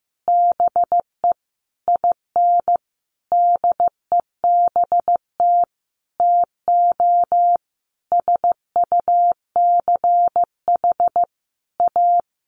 Morse code, heard as a keyed tone, about 15 wpm, 700 Hz. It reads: BE IN DEBT TO SUCH A
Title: Sentences from Top 1300 Words 15wpm